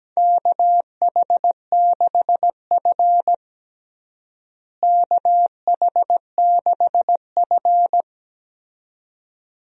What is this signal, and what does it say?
Morse code, heard as a keyed tone, about 17 wpm, 700 Hz. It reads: KH6F KH6F
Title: USA Call Signs - 4 Characters - Repeated 2x - 17wpm